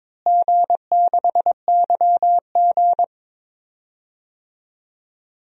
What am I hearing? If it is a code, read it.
Morse code, heard as a keyed tone, about 22 wpm, 700 Hz. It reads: G6YG
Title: DXCC-100 Call Signs - 4 Characters - Repeated 2x - 22wpm